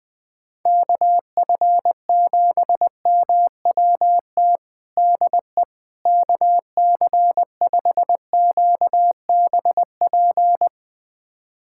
Morse code, heard as a keyed tone, about 20 wpm, 700 Hz. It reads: KF7MWT DE KC5QBP
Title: USA Callsign DE Callsign - 6 Characters - 20wpm